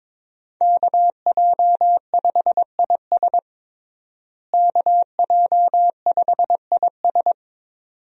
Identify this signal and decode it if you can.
Morse code, heard as a keyed tone, about 22 wpm, 700 Hz. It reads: KJ5IS KJ5IS
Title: USA Call Signs - 5 Characters - Repeated 2x - 22wpm